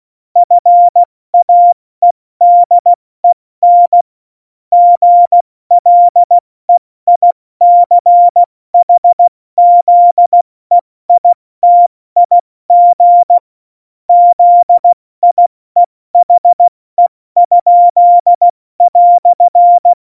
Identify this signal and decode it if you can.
Morse code, heard as a keyed tone, about 16 wpm, 700 Hz. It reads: FAEDEN GLEICHZEITIG ZIEHE?"
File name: Morsekurs / Witze / Witz 20. November 2018, 70 F80